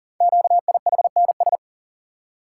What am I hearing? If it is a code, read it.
Morse code, heard as a keyed tone, about 38 wpm, 690 Hz. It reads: QIHNS
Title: Code Groups - 5 Letters - 40wpm